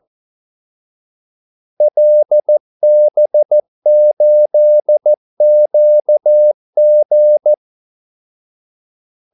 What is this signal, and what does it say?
Morse code, heard as a keyed tone, about 14 wpm, 600 Hz. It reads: LB8QG
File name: Callsigns-14wpm